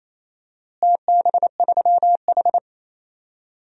Morse code, heard as a keyed tone, about 28 wpm, 700 Hz. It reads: TB3H
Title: DXCC-100 Call Signs - 4 Characters - Repeated 3x - 28wpm